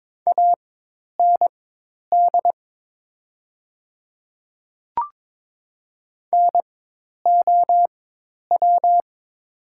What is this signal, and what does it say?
Morse code, heard as a keyed tone, about 21 wpm, spaced as for 10 wpm, 700 Hz. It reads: AND NOW
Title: 4x Character Spacing - Top 100 Words - 22wpm